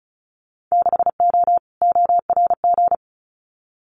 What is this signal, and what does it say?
Morse code, heard as a keyed tone, about 34 wpm, 700 Hz. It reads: 6O ORG
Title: Winter Field Day Exchanges Encore 35wpm